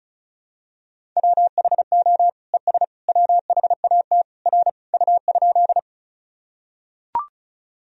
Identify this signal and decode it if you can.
Morse code, heard as a keyed tone, about 34 wpm, 700 Hz. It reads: WHO ES WHAT R U?
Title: Sentences from Top 100 Words and Abbreviations 35wpm